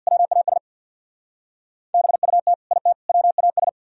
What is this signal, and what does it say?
Morse code, heard as a keyed tone, about 59 wpm, 700 Hz. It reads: WAS BUT IT WAS
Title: Speed-Racing - Sets of 3 Words 50wpm